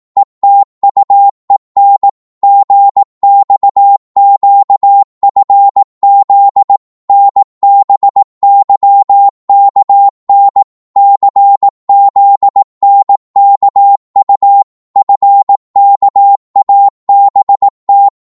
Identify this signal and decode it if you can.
Morse code, heard as a keyed tone, about 18 wpm, 800 Hz. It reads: ETUEN GXQFZ NBYKN CZNKU FKABT
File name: Morse / ltr-18wpm